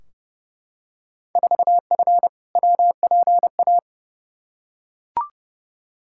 Morse code, heard as a keyed tone, about 31 wpm, 700 Hz. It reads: 4F WPA
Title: Field Day Exchanges - 30wpm